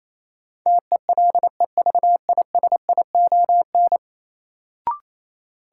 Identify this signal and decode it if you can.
Morse code, heard as a keyed tone, about 28 wpm, 700 Hz. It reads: TELEVISION
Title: Sentences from Top 1000 Words - Review Part 4 - New Word Drill - 28wpm